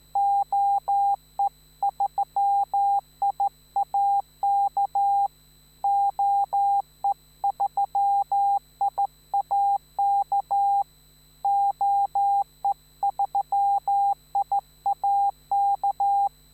Morse code, heard as a keyed tone, about 14 wpm, 800 Hz. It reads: OE3IAK OE3IAK OE3IAK
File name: QRMFragezeichen